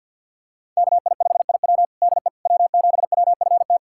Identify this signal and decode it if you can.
Morse code, heard as a keyed tone, about 49 wpm, 700 Hz. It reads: KI5IW DE W7WFT
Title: USA Callsign DE Callsign 50wpm